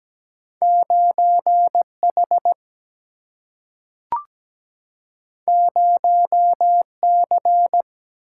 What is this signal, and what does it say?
Morse code, heard as a keyed tone, about 17 wpm, 700 Hz. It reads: 9 H 0 C
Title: Single-Letter Single-Number 17wpm